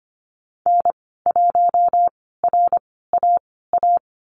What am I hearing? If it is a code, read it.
Morse code, heard as a keyed tone, about 25 wpm, spaced as for 16 wpm, 690 Hz. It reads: N1RAA
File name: Lesson 15g - Number 5 - Call Signs 25wpm